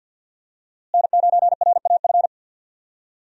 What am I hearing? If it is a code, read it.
Morse code, heard as a keyed tone, about 47 wpm, 700 Hz. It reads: N9RAF
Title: USA Call Signs - 5 Characters - Repeated 2x - 50wpm